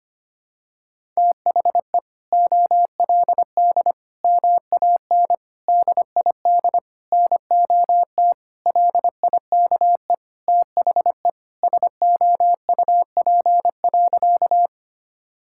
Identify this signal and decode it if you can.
Morse code, heard as a keyed tone, about 25 wpm, 700 Hz. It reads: THE OLD MAN DID NOT LIKE THE SOUP.
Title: Sentences from Top 1000 Words - Review Part 6 - 25wpm